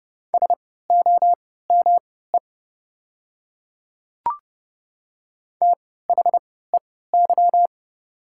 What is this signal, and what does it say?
Morse code, heard as a keyed tone, about 30 wpm, spaced as for 17 wpm, 700 Hz. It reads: SOME THEY
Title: 3x Character Spacing - Top 100 Words - 30wpm